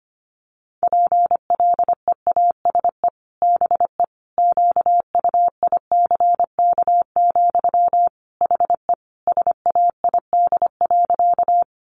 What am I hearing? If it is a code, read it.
Morse code, heard as a keyed tone, about 25 wpm, 700 Hz. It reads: PLEASE BE QUICK, HE SAID.
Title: Sentences from Top 1500 Words 25wpm